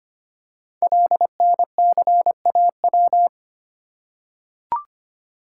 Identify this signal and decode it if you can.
Morse code, heard as a keyed tone, about 25 wpm, 700 Hz. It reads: LNCAW
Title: Code Groups - 5 Letters - 25wpm